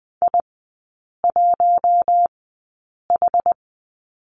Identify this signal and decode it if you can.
Morse code, heard as a keyed tone, about 20 wpm, spaced as for 8 wpm, 700 Hz. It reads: I1H
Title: 093 Lesson 16e - Letter c - Three Characters 20wpm